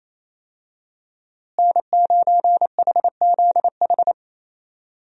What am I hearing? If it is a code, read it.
Morse code, heard as a keyed tone, about 27 wpm, 700 Hz. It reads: N9HZH
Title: USA Call Signs - 5 Characters - Repeated 3x - 28wpm